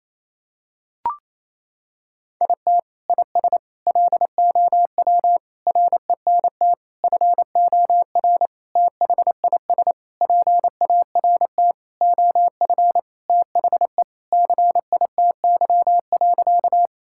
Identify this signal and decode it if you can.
Morse code, heard as a keyed tone, about 27 wpm, 700 Hz. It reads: IT IS LOW RENT FOR THIS PART OF THE CITY.
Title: Sentences from Top 1000 Words - Review Part 5 - 28wpm